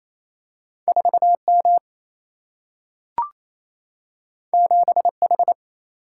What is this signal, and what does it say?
Morse code, heard as a keyed tone, about 28 wpm, 700 Hz. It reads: 4 M 7 H
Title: Single-Letter Single-Number 28wpm